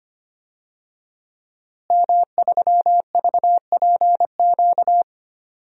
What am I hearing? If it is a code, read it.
Morse code, heard as a keyed tone, about 25 wpm, 700 Hz. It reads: M3VPQ
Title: DXCC-100 Call Signs - 5 Characters - Repeated 3x - 25wpm